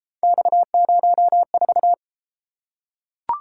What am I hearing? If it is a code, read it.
Morse code, heard as a keyed tone, about 30 wpm, 700 Hz. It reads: X04